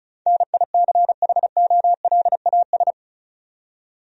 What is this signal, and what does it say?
Morse code, heard as a keyed tone, about 36 wpm, 700 Hz. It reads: NICHOLAS
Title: USA Common Names - 35wpm